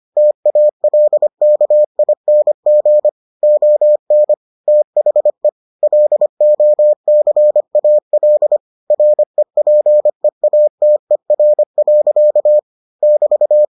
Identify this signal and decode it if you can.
Morse code, heard as a keyed tone, about 25 wpm, 600 Hz. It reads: TALKING ON THE LOCAL REPEATER. <BT>